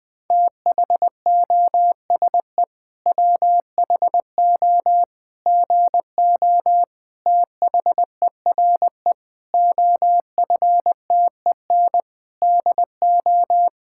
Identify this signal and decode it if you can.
Morse code, heard as a keyed tone, about 20 wpm, 700 Hz. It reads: THOSE WHO GO THERE OFTEN DO
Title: Sentences from Top 1000 Words - Review Part 5 - 20wpm